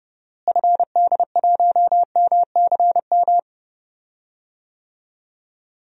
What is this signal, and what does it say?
Morse code, heard as a keyed tone, about 30 wpm, 700 Hz. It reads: FD1MCM
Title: DXCC-100 Call Signs - 6 Characters - Repeated 3x - 30wpm